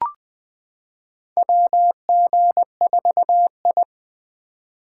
Morse code, heard as a keyed tone, about 20 wpm, 700 Hz. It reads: WG4I
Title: 2023 SOTA Chaser Call Signs - 20wpm